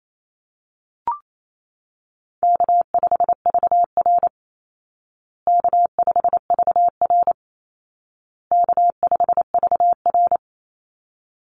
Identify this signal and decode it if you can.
Morse code, heard as a keyed tone, about 28 wpm, 700 Hz. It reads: K5VR K5VR K5VR
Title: Call Signs Repeated 2x - 28wpm